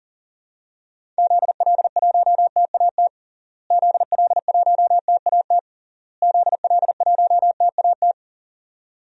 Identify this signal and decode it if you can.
Morse code, heard as a keyed tone, about 37 wpm, 700 Hz. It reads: ZL1TAT ZL1TAT ZL1TAT
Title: DXCC-100 Call Signs - 6 Characters - Repeated 3x - 40wpm